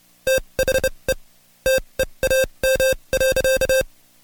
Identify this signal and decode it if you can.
Morse code, heard as a keyed tone, about 29 wpm, spaced as for 23 wpm, 530 Hz. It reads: THE TEAM.
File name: Code Quick / AudioDownld2 / Seg138-Stepper11 23WPM